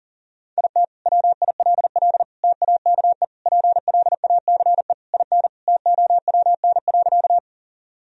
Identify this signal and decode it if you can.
Morse code, heard as a keyed tone, about 40 wpm, 700 Hz. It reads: IT WILL TAKE PLACE IN TOWN.
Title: Sentences from Top 1000 Words - Review Part 2 - 40wpm